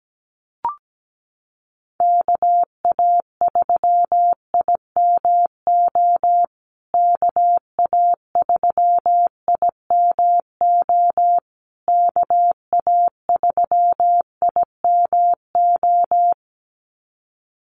Morse code, heard as a keyed tone, about 17 wpm, 700 Hz. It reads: KA3IMO KA3IMO KA3IMO
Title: USA Call Signs Repeated 3x 17wpm